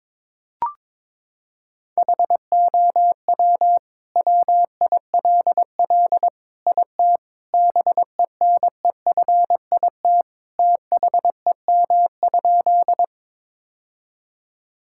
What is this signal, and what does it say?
Morse code, heard as a keyed tone, about 22 wpm, 700 Hz. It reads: HOW WILL IT BENEFIT THEM?
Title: Sentences from Top 1000 Words - Review Part 6 - 22wpm